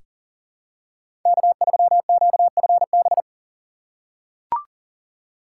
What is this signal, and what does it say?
Morse code, heard as a keyed tone, about 40 wpm, 700 Hz. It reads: K3QFB
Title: USA Call Signs - 5 Characters - 2x Character Spacing - 40wpm